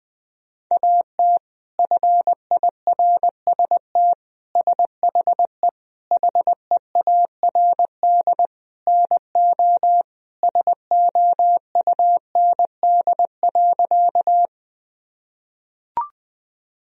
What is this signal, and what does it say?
Morse code, heard as a keyed tone, about 20 wpm, 700 Hz. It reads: AT FIRST SHE HEARD NO SOUND.
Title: Sentences from Top 1000 Words - Review Part 1 - 20wpm